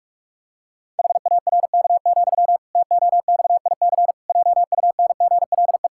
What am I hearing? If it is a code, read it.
Morse code, heard as a keyed tone, about 46 wpm, 700 Hz. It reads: HARK, TOXIC JUNGLE
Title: Pangram Challenge v2 - 45wpm